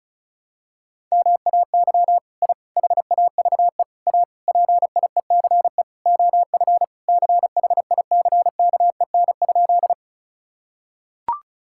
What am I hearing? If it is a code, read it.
Morse code, heard as a keyed tone, about 36 wpm, 700 Hz. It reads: MAY I HAVE A PIECE OF CHICKEN?
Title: Sentences from Top 1000 Words - Review Part 5 - 35wpm